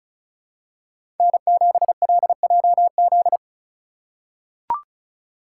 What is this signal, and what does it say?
Morse code, heard as a keyed tone, about 34 wpm, 700 Hz. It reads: N7LJZ
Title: USA Call Signs - 5 Characters - 2x Character Spacing - 35wpm